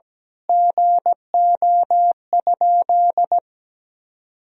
Morse code, heard as a keyed tone, about 17 wpm, 700 Hz. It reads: GO?
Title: Sentences from Top 1000 Words - Review Part 4 - 17wpm